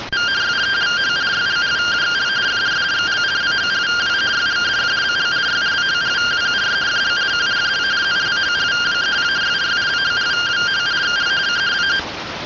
Two-tone teleprinter signal, about 170 Hz shift.
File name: findings / axial iodine 3mm 205.gen-32 rtty rx